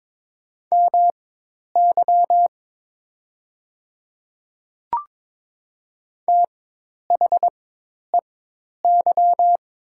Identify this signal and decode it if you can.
Morse code, heard as a keyed tone, about 22 wpm, spaced as for 10 wpm, 700 Hz. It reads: MY THEY
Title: 4x Character Spacing - Top 100 Words - 22wpm